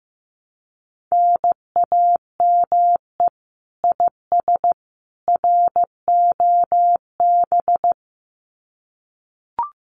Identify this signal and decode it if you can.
Morse code, heard as a keyed tone, about 15 wpm, 700 Hz. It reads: NAME IS ROB
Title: Common QSO Phrases 15wpm